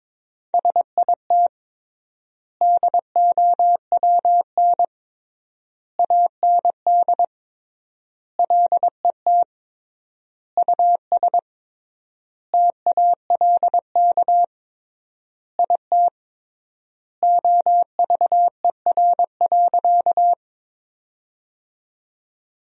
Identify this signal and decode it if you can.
Morse code, heard as a keyed tone, about 22 wpm, 700 Hz. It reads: SIT DOWN AND LET US TALK IT OVER.
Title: 3x Word-Spacing - Sentences from Top 500 Words - 22wpm